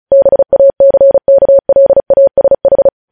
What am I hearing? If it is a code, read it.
Morse code, heard as a keyed tone, about 36 wpm, 550 Hz. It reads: BACKLASH
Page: news headlines in morse code at 35 wpm